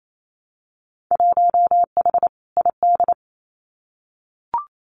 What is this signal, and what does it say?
Morse code, heard as a keyed tone, about 28 wpm, 700 Hz. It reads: 1H ID
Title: Winter Field Day Exchanges 28wpm